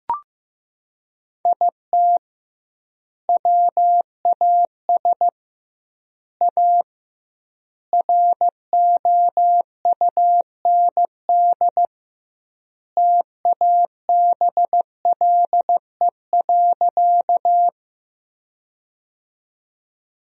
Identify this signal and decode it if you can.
Morse code, heard as a keyed tone, about 16 wpm, 700 Hz. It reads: IT WAS A ROUND TABLE.
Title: 2x Word-Spacing - Sentences from Top 400 Words - 15wpm